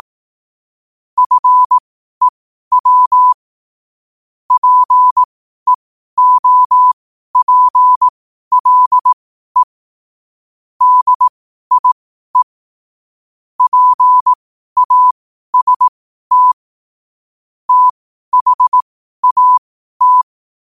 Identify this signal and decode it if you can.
Morse code, heard as a keyed tone, about 18 wpm, spaced as for 12 wpm, 1000 Hz. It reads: FEW PEOPLE DIE PAST THAT
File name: qotdfiles / 12wpm